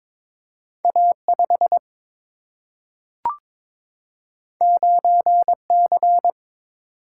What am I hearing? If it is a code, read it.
Morse code, heard as a keyed tone, about 22 wpm, 700 Hz. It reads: A 5 9 C
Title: Single-Letter Single-Number 22wpm